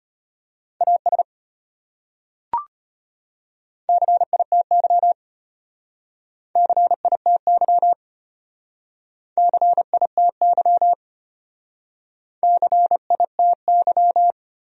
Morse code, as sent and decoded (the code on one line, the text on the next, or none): .- ... / -.-. .. - -.-- / -.-. .. - -.-- / -.-. .. - -.-- / -.-. .. - -.--
AS CITY CITY CITY CITY